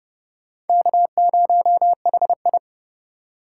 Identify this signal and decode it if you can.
Morse code, heard as a keyed tone, about 30 wpm, 700 Hz. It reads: K0HI